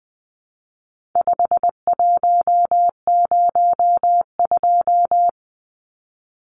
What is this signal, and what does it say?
Morse code, heard as a keyed tone, about 20 wpm, 700 Hz. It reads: 5102
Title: Numbers - 4-Digit 20wpm